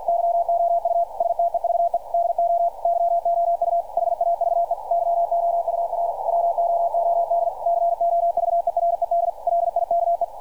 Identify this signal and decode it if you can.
Morse code, heard as a keyed tone, about 50 wpm, 690 Hz, with radio noise. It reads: YOU STEVE DO YOU HAVE YOUR WOOD COVERED LIKE
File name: archived / CRSnet